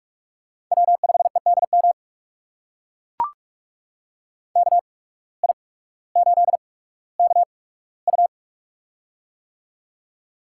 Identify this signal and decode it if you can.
Morse code, heard as a keyed tone, about 44 wpm, spaced as for 19 wpm, 700 Hz. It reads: W5EDM KI8KU
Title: USA Call Signs - 5 Characters - 8x Character Spacing - 45wpm